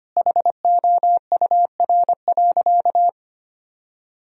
Morse code, heard as a keyed tone, about 25 wpm, 700 Hz. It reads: HOUR.
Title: Sentences from Top 1000 Words - Review Part 1 - 25wpm